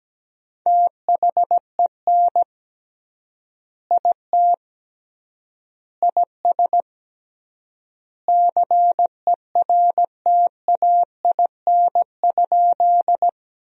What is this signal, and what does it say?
Morse code, heard as a keyed tone, about 17 wpm, 700 Hz. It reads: THEN IT IS CERTAIN?
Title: 3x Word-Spacing - Sentences from Top 500 Words - 17wpm